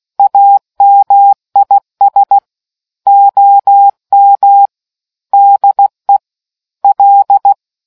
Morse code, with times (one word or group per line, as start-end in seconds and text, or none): 0.2-2.4 AMIS
3.1-4.7 OM
5.3-6.2 DE
6.8-7.5 L